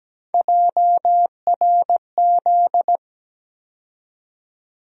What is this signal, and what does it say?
Morse code, heard as a keyed tone, about 17 wpm, 700 Hz. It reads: JRZ
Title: USA Call Sign Postfixes - 3 Letters 17wpm